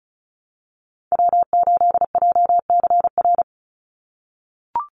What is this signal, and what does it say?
Morse code, heard as a keyed tone, about 34 wpm, 700 Hz. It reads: W8JCR
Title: USA Call Signs 35wpm